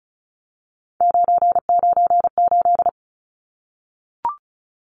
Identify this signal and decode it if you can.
Morse code, heard as a keyed tone, about 34 wpm, 700 Hz. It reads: 998
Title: Numbers - 3-Digit 35wpm